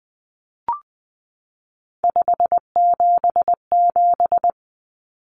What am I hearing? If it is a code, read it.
Morse code, heard as a keyed tone, about 20 wpm, 700 Hz. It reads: 577
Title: Numbers - 3-Digit 20wpm